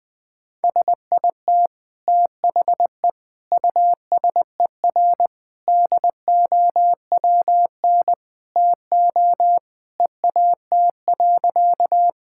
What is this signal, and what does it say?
Morse code, heard as a keyed tone, about 20 wpm, 700 Hz. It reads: SIT THE USER DOWN TO EAT.
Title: Sentences from Top 1000 Words - Review Part 5 - 20wpm